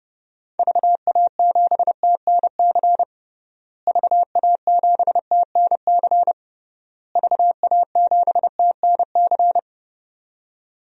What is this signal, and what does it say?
Morse code, heard as a keyed tone, about 30 wpm, 700 Hz. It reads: VA7TNC VA7TNC VA7TNC
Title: DXCC-100 Call Signs - 6 Characters - Repeated 3x - 30wpm